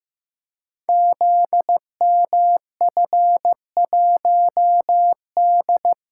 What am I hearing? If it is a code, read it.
Morse code, heard as a keyed tone, about 15 wpm, 700 Hz. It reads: ZMF1D
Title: Code Groups - 5 Letter:Numbers - 15wpm